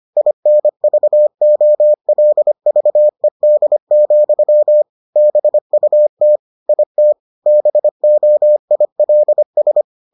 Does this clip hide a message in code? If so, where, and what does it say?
Yes: Morse code from 0.2 to 9.8 s, heard as a keyed tone, about 25 wpm, 600 Hz. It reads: INVOLVED, BUT IT BOILS